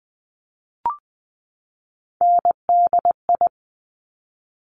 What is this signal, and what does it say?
Morse code, heard as a keyed tone, about 20 wpm, 700 Hz. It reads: NDI
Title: Top Three Letter Combinations in Words 20wpm